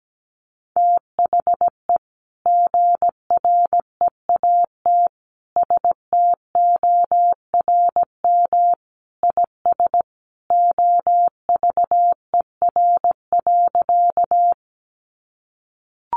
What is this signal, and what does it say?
Morse code, heard as a keyed tone, about 17 wpm, 700 Hz. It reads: THE GREAT STORM IS OVER.
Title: Sentences from Top 1500 Words 17wpm